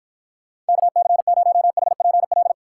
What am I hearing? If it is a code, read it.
Morse code, heard as a keyed tone, about 53 wpm, 700 Hz. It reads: KC0HPL